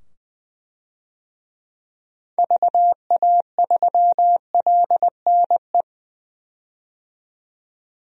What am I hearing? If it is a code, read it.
Morse code, heard as a keyed tone, about 21 wpm, 700 Hz. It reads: VA3LNE
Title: DXCC-100 Call Signs - 6 Characters - 20wpm